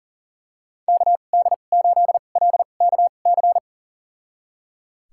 Morse code, heard as a keyed tone, about 37 wpm, spaced as for 28 wpm, 700 Hz. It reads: KD8LKC
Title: USA Call Signs - 6 Characters - 2x Character Spacing - 40wpm